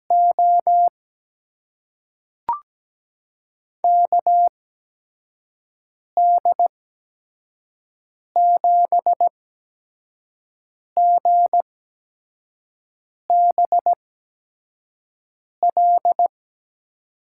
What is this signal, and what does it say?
Morse code, heard as a keyed tone, about 17 wpm, spaced as for 5 wpm, 700 Hz. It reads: O KD7GBL
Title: USA Call Signs - 6 Characters - 8x Character Spacing - 17wpm